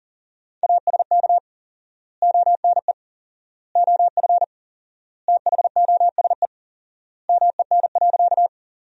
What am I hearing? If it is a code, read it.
Morse code, heard as a keyed tone, about 37 wpm, 700 Hz. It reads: ASK ONE OF THOSE MEN.
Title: 4x Word-Spacing - Sentences from Top 300 Words - 40wpm